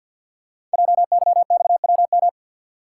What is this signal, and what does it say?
Morse code, heard as a keyed tone, about 46 wpm, 700 Hz. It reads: JYXWM
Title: Code Groups - 5 Letters - 50wpm